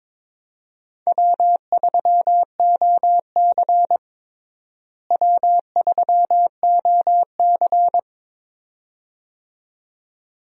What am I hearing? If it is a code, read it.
Morse code, heard as a keyed tone, about 21 wpm, 700 Hz. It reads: W3OC W3OC
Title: USA Call Signs - 4 Characters - Repeated 2x - 22wpm